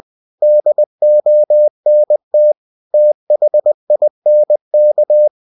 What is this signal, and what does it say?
Morse code, heard as a keyed tone, about 20 wpm, 600 Hz. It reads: DONT THINK
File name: cw-books / winnie-the-pooh / winnie-20wpm-page-0033